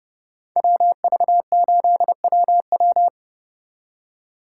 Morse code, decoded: WV8WW

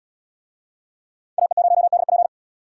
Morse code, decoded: AE0DP

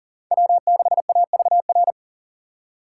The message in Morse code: W6AVR

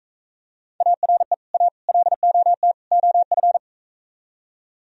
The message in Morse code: ARE A LOT OF

